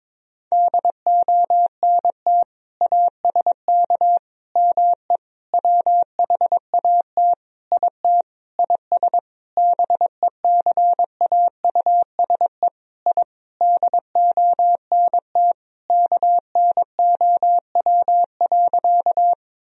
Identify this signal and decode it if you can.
Morse code, heard as a keyed tone, about 22 wpm, 700 Hz. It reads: DONT ASK ME WHAT IT IS BECAUSE I DONT KNOW.
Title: Sentences from Top 1000 Words - Review Part 3 - 22wpm